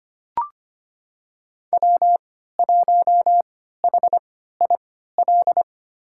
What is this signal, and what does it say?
Morse code, heard as a keyed tone, about 25 wpm, spaced as for 14 wpm, 700 Hz. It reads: W1HIL
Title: USA Call Signs - 3x Character Spacing - 25wpm